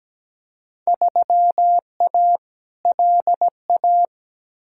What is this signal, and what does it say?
Morse code, heard as a keyed tone, about 17 wpm, 700 Hz. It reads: 3A LA
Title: Field Day Exchanges - 17wpm